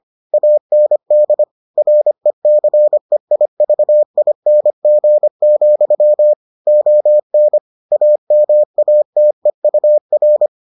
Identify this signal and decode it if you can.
Morse code, heard as a keyed tone, about 25 wpm, 600 Hz. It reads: AND RECEIVING, ON AMATEUR